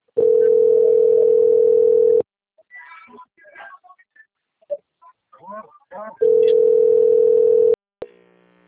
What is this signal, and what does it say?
Ringback tone (North America), 440 + 480 Hz.